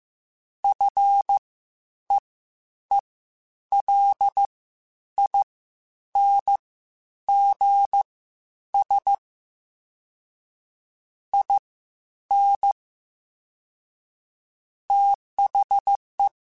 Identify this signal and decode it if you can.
Morse code, heard as a keyed tone, about 15 wpm, spaced as for 9 wpm, 800 Hz. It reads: FEELINGS IN THE
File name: Quotes Spaced / Quotes Spaced-mp3s / ChildAsleep sp @13